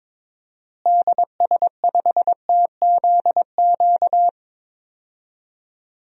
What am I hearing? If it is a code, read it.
Morse code, heard as a keyed tone, about 22 wpm, 700 Hz. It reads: DS5TZQ
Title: DXCC-100 Call Signs - 6 Characters - Repeated 3x - 22wpm